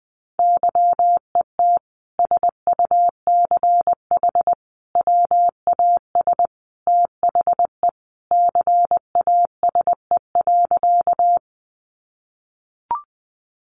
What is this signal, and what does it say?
Morse code, heard as a keyed tone, about 20 wpm, 700 Hz. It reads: YET SUCH WAS THE CASE.
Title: Sentences from Top 700 Words 20wpm